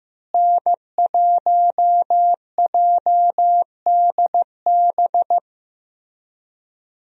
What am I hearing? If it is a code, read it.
Morse code, heard as a keyed tone, about 15 wpm, 700 Hz. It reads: N1JDB
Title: USA Call Signs - 5 Characters - Repeated 2x - 15wpm